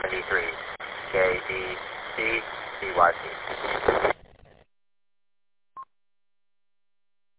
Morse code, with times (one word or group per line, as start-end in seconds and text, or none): none